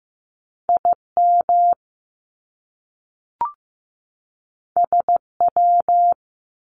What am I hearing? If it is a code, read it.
Morse code, heard as a keyed tone, about 15 wpm, 700 Hz. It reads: IM SW
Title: Top Two Letter Combinations in Words 15wpm